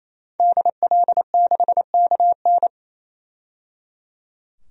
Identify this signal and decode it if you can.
Morse code, heard as a keyed tone, about 28 wpm, 700 Hz. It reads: DL6KN
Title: DXCC-100 Call Signs - 5 Characters - 28wpm